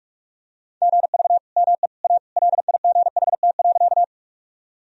Morse code, heard as a keyed tone, about 46 wpm, 700 Hz. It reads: GV ME A LIGHT.
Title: Sentences from Top 200 Words and Abbreviations 45wpm